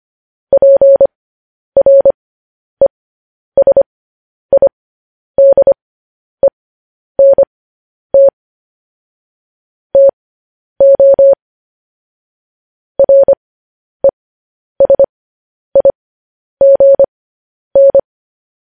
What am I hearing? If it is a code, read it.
Morse code, heard as a keyed tone, about 25 wpm, spaced as for 10 wpm, 550 Hz. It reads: PRESIDENT TO RESIGN